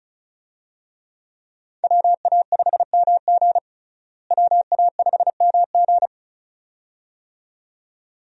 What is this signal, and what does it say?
Morse code, heard as a keyed tone, about 36 wpm, 700 Hz. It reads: WA5MG WA5MG
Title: USA Call Signs - 5 Characters - Repeated 2x - 35wpm